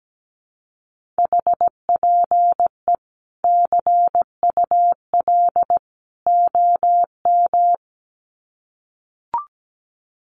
Morse code, heard as a keyed tone, about 17 wpm, 700 Hz. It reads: HPE CUL OM
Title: Common QSO Phrases 17wpm